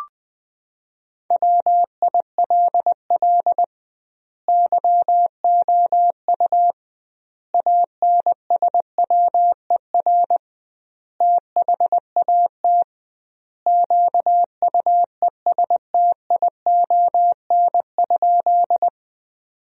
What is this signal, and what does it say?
Morse code, heard as a keyed tone, about 20 wpm, 700 Hz. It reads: WILL YOU ANSWER THAT QUESTION?
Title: 2x Word-Spacing - Sentences from Top 400 Words - 20wpm